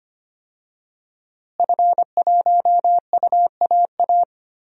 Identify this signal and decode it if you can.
Morse code, heard as a keyed tone, about 25 wpm, 700 Hz. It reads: F1UAA
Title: DXCC-100 Call Signs - 5 Characters - Repeated 3x - 25wpm